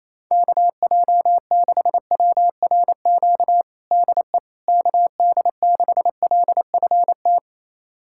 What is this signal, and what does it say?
Morse code, heard as a keyed tone, about 28 wpm, 700 Hz. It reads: KJ6WRQ DE KD6LFT
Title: USA Callsign DE Callsign - 6 Characters - 28wpm